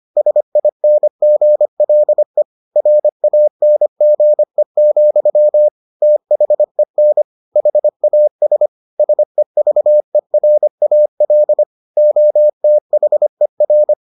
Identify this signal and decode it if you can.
Morse code, heard as a keyed tone, about 25 wpm, 600 Hz. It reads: SINGLE RANGE, THEN HAS SEVERAL OTHER